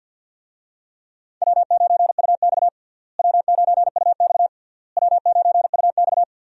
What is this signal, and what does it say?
Morse code, heard as a keyed tone, about 48 wpm, 700 Hz. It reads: W9UX W9UX W9UX
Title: USA Call Signs - 4 Characters - Repeated 3x - 50wpm